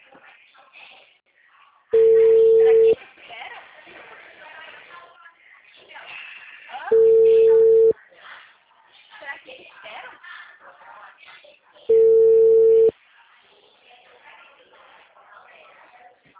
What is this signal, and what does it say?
Ringback tone (China, 450 Hz), with speech.